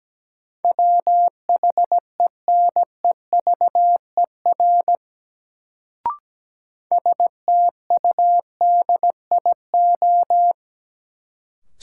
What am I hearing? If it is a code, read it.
Morse code, heard as a keyed tone, about 17 wpm, 700 Hz. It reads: WHENEVER STUDIO
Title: Sentences from Top 1200 Words - New Word Drill - 17wpm